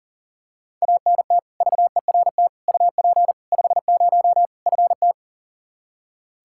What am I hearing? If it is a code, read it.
Morse code, heard as a keyed tone, about 40 wpm, 700 Hz. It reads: ANT VERT UP 50 FT
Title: Practice QSO - Exchange 2 - Info - 40wpm